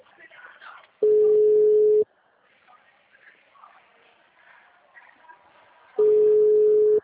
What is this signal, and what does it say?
Ringback tone (Europe, 425 Hz), with speech.